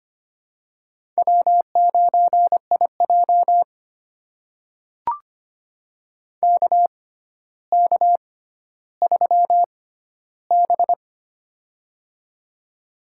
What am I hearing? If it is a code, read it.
Morse code, heard as a keyed tone, about 25 wpm, spaced as for 13 wpm, 700 Hz. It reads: W9IJ KK3B
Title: USA Call Signs - 4 Characters - 6x Character Spacing - 25wpm